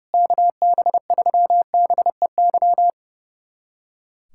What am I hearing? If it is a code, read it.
Morse code, heard as a keyed tone, about 30 wpm, 700 Hz. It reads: KB3BEY
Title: USA Callsign DE Callsign - 6 Characters - 30wpm